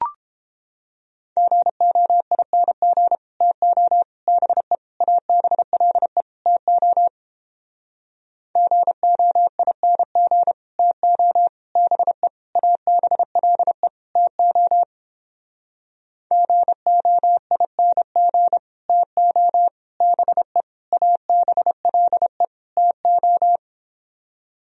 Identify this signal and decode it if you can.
Morse code, heard as a keyed tone, about 29 wpm, 700 Hz. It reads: GOING TO BE ABLE TO GOING TO BE ABLE TO GOING TO BE ABLE TO